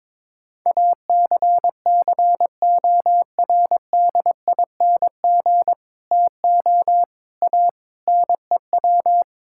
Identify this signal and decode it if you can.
Morse code, heard as a keyed tone, about 22 wpm, 700 Hz. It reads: ACCORDING TO A NEW
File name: Sentences with 7-Word Phrases - 22wpm